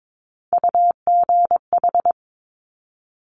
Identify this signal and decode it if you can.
Morse code, heard as a keyed tone, about 22 wpm, 700 Hz. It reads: UGH
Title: Top Three Letter Combinations in Words 22wpm